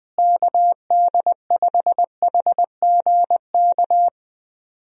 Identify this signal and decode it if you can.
Morse code, heard as a keyed tone, about 20 wpm, 700 Hz. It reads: KD5HGK
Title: USA Call Signs - 6 Characters - Repeated 3x - 20wpm